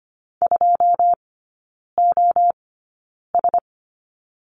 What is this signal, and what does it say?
Morse code, heard as a keyed tone, about 25 wpm, spaced as for 9 wpm, 690 Hz. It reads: 2OS
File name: Lesson 14e - Number 2 - Three Characters 25wpm